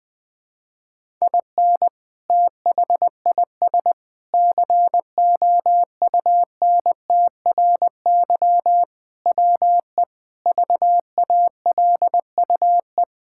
Morse code, heard as a keyed tone, about 21 wpm, 700 Hz. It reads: IN THIS COUNTRY WE VALUE
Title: Sentences with 3-Word Phrases - Spoken Beginning - 20wpm